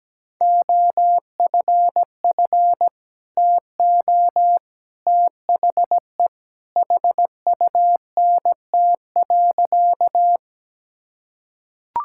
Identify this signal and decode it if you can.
Morse code, heard as a keyed tone, about 17 wpm, 700 Hz. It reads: OFF TO THE HUNT.
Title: Sentences from Top 1600 Words 17wpm